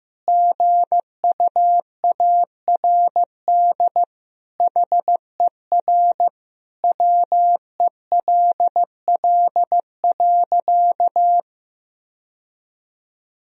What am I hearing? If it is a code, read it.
Morse code, heard as a keyed tone, about 16 wpm, 700 Hz. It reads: GUARD HER WELL.
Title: Sentences from Top 1000 Words - Review Part 4 - 15wpm